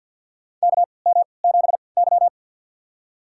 Morse code, heard as a keyed tone, about 48 wpm, spaced as for 28 wpm, 700 Hz. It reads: KM7Y
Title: USA Call Signs - 4 Characters - 3x Character Spacing - 50wpm